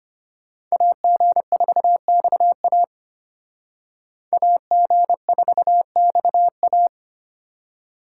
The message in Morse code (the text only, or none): AG4XA AG4XA